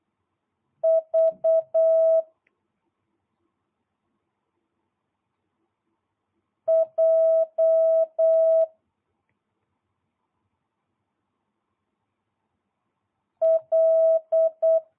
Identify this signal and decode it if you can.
Morse code, heard as a keyed tone, about 8 wpm, 650 Hz. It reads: V J L